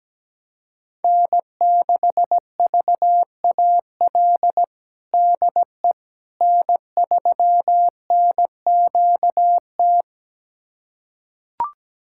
Morse code, heard as a keyed tone, about 17 wpm, 700 Hz. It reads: N6VAL DE N3NQT
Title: USA Callsign DE Callsign - 5 Characters - 17wpm